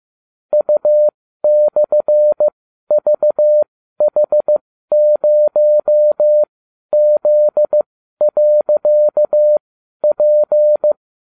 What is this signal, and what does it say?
Morse code, heard as a keyed tone, about 15 wpm, spaced as for 12 wpm, 600 Hz. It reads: U/VH0Z.P
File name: cw / ubacw30 3